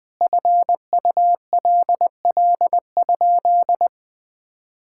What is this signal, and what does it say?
Morse code, heard as a keyed tone, about 20 wpm, 700 Hz. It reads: FULL?
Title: Sentences from Top 1600 Words 20wpm